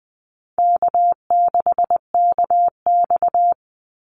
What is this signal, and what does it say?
Morse code, heard as a keyed tone, about 20 wpm, 700 Hz. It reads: K6KX
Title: Easy Call Signs Repeated 3x - 20wpm